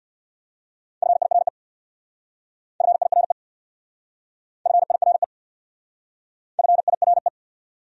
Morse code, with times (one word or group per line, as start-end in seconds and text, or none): 1.0-1.5 FIRE
2.8-3.3 FIRE
4.7-5.2 FIRE
6.6-7.3 FIRE